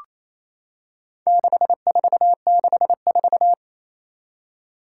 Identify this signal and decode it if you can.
Morse code, heard as a keyed tone, about 28 wpm, 700 Hz. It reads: 6464
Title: Four Challenging Letter-Number Combinations 28wpm